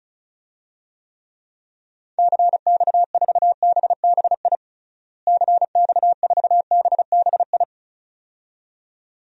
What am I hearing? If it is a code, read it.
Morse code, heard as a keyed tone, about 36 wpm, 700 Hz. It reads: CX4BBI CX4BBI
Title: DXCC-100 Call Signs - 6 Characters - Repeated 2x - 35wpm